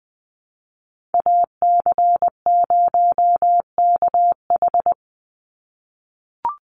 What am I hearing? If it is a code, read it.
Morse code, heard as a keyed tone, about 20 wpm, 700 Hz. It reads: AC0KH
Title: USA Call Signs 20wpm